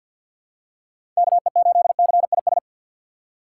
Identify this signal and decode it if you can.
Morse code, heard as a keyed tone, about 53 wpm, 700 Hz. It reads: KE8CIS
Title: USA Call Signs - 6 Characters - 3x Character Spacing - 50wpm